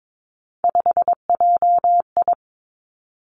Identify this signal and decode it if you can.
Morse code, heard as a keyed tone, about 22 wpm, 700 Hz. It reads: 5JI
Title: Three Challenging Letter-Number Combinations 22wpm